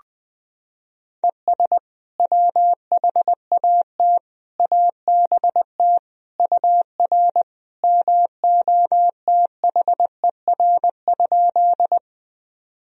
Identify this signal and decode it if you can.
Morse code, heard as a keyed tone, about 20 wpm, 700 Hz. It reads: ES WHAT ABT UR MOTHER?
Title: Sentences from Top 200 Words and Abbreviations 20wpm